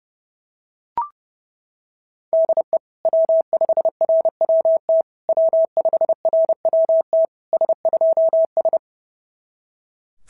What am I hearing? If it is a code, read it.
Morse code, heard as a keyed tone, about 30 wpm, 690 Hz. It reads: DE W5RWT W5RWT S2S
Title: SOTA QSO Practice 30wpm